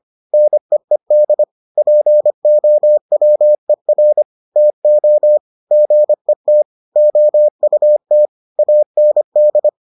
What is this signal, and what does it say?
Morse code, heard as a keyed tone, about 25 wpm, 600 Hz. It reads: NEED POWER TO GET OUT AND